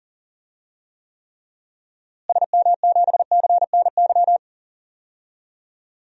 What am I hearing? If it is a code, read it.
Morse code, heard as a keyed tone, about 40 wpm, 700 Hz. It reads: SM7CNY
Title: DXCC-100 Call Signs - 6 Characters - Repeated 3x - 40wpm